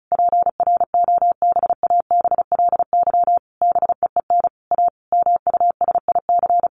Morse code, heard as a keyed tone, about 34 wpm, 700 Hz. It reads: PROBABLY BEEN A MUSIC
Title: Sentences from Top 700 Words 35wpm